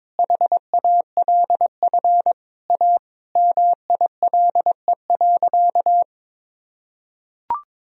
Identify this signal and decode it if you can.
Morse code, heard as a keyed tone, about 22 wpm, 700 Hz. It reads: HALF A MILE.
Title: Sentences from Top 1000 Words - Review Part 1 - 22wpm